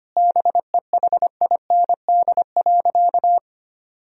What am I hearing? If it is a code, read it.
Morse code, heard as a keyed tone, about 25 wpm, 700 Hz. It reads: BEHIND.
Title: Sentences from Top 1000 Words - Review Part 2 - 25wpm